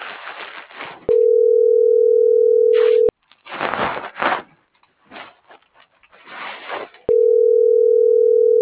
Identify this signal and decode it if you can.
Ringback tone (North America), 440 + 480 Hz.